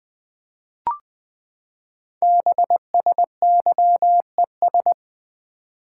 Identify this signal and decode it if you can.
Morse code, heard as a keyed tone, about 20 wpm, 700 Hz. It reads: BSYES
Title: Code Groups - 5 Letters - 20wpm